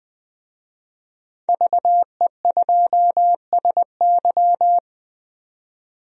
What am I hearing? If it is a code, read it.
Morse code, heard as a keyed tone, about 21 wpm, 700 Hz. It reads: VE2SY
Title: DXCC-100 Call Signs - 5 Characters - 20wpm